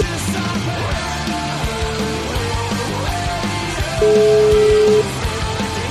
Ringback tone (Europe, 425 Hz), with noise.